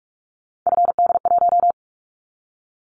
Morse code, heard as a keyed tone, about 46 wpm, 700 Hz. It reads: FD1
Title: Callsign Prefixes for Top 50 DXCC - 45wpm